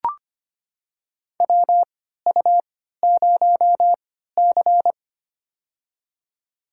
Morse code, heard as a keyed tone, about 25 wpm, spaced as for 14 wpm, 700 Hz. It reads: WU0C
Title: USA Call Signs - 4 Characters - 3x Character Spacing - 25wpm